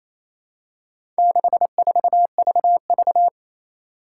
Morse code, heard as a keyed tone, about 27 wpm, 700 Hz. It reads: 64VV
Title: Four Challenging Letter-Number Combinations 28wpm